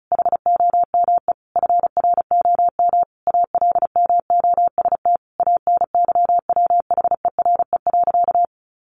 Morse code, heard as a keyed tone, about 34 wpm, 700 Hz. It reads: HOME FROM ALMOST ANYWHERE.
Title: Sentences from Top 1400 Words 35wpm